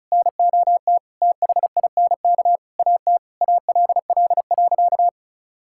Morse code, heard as a keyed tone, about 36 wpm, 700 Hz. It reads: NOT THINK AT ALL.
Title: Sentences from Top 1600 Words 35wpm